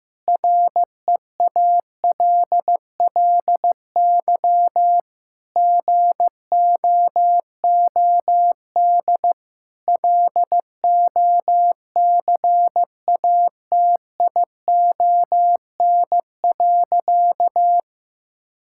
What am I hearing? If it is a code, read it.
Morse code, heard as a keyed tone, about 16 wpm, 700 Hz. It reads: REALLY GOOD LOCATION.
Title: Sentences from Top 1000 Words - Review Part 6 - 15wpm